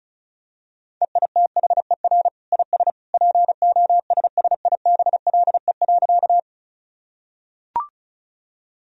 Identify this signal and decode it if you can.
Morse code, heard as a keyed tone, about 36 wpm, 700 Hz. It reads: EITHER IS POSSIBLE.
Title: Sentences from Top 1000 Words - Review Part 4 - 35wpm